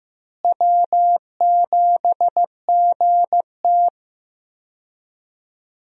Morse code, heard as a keyed tone, about 15 wpm, 700 Hz. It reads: W7GT
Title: USA Call Signs - 4 Characters - Repeated 3x - 15wpm